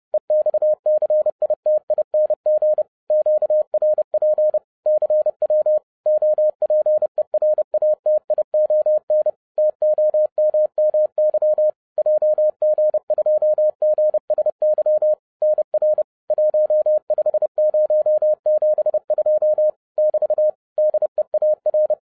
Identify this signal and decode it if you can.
Morse code, heard as a keyed tone, about 30 wpm, 610 Hz. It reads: EXCITING QRP CW OPERATION TOMMY JG2GSY NR 15072 <BT> DEAR
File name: 2011-07-mp3 / n40 EXCITING QRP CW OPERATION 30wpm